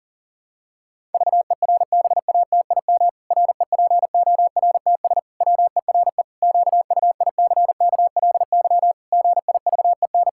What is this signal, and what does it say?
Morse code, heard as a keyed tone, about 40 wpm, 700 Hz. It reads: VERBATIM REPORTS WERE QUICKLY GIVEN
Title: Pangram Challenge v2 - 40wpm